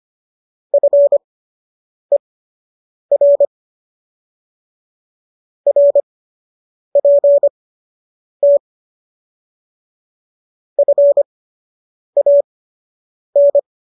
Morse code, heard as a keyed tone, about 25 wpm, spaced as for 8 wpm, 570 Hz. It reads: FER RPT FAN